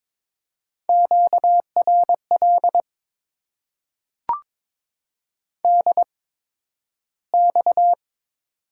Morse code, heard as a keyed tone, about 22 wpm, spaced as for 12 wpm, 700 Hz. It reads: QRL DX
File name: Top 90 QSO Elements - 8x Character Spacing - 22wpm